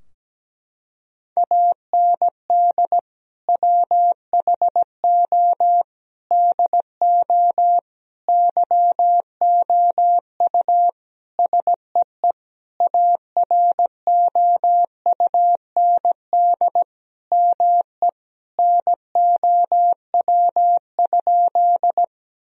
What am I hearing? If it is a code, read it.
Morse code, heard as a keyed tone, about 17 wpm, 700 Hz. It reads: AND WHO DO YOU SEE AROUND ME NOW?
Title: Sentences from Top 1000 Words - Review Part 3 - 17wpm